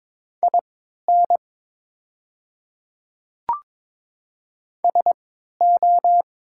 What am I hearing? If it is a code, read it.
Morse code, heard as a keyed tone, about 22 wpm, spaced as for 12 wpm, 700 Hz. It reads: IN SO
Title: 3x Character Spacing - Top 100 Words - 22wpm